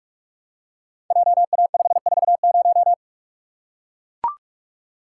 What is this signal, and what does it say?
Morse code, heard as a keyed tone, about 42 wpm, 700 Hz. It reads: JA540
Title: Code Groups - 5 Letter:Numbers - 45wpm